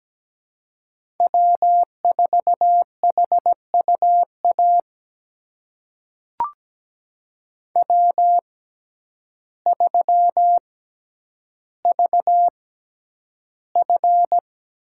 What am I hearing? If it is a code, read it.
Morse code, heard as a keyed tone, about 17 wpm, spaced as for 9 wpm, 700 Hz. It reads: W4HUA W3VF
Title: USA Call Signs - 6x Character Spacing - 17wpm